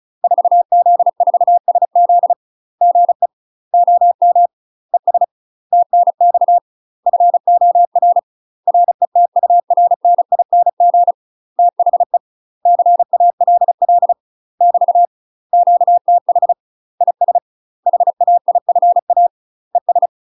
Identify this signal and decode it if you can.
Morse code, heard as a keyed tone, about 36 wpm, 700 Hz. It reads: 4Z4SZ GE OM ES TNX FOR RETURNING THE CALL <BT> QTH IS HAIFA ES
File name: UK MASTER QSOs TRAINING / 35WPM / UK MASTER QSOs 0020